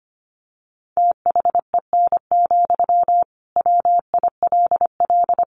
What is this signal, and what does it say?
Morse code, heard as a keyed tone, about 25 wpm, 700 Hz. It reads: THEN, WILL
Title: Sentences from Top 1400 Words 25wpm